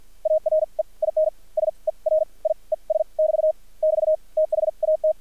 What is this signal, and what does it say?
Morse code, heard as a keyed tone, about 50 wpm, spaced as for 37 wpm, 630 Hz. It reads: ARE IN SERIES <BT> <BT> THAT
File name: archived / CRSnet